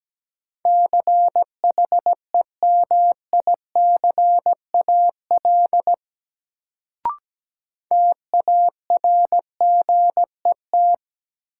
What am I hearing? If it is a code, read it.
Morse code, heard as a keyed tone, about 17 wpm, 700 Hz. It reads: CHEMICAL TARGET
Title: Sentences from Top 1200 Words - New Word Drill Alt - 17wpm